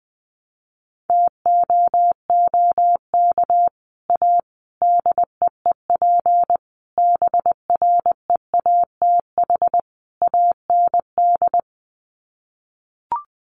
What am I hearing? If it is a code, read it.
Morse code, heard as a keyed tone, about 19 wpm, 700 Hz. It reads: TOOK A DEEP BREATH AND
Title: Sets of 5 Words Encore 20wpm